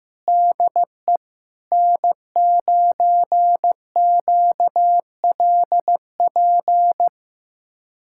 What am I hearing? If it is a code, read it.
Morse code, heard as a keyed tone, about 15 wpm, 700 Hz. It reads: DE N9QLP
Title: USA Callsign DE Callsign - 5 Characters - 15wpm